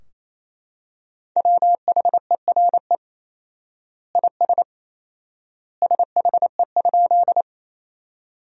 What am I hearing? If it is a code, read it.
Morse code, heard as a keyed tone, about 27 wpm, 700 Hz. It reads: WHERE IS SHE?
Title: 4x Word-Spacing - Sentences from Top 200 Words - 28wpm